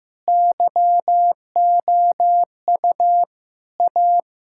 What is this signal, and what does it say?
Morse code, heard as a keyed tone, about 16 wpm, 700 Hz. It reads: YOU A
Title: Sentences from Top 1000 Words - Review Part 6 - 15wpm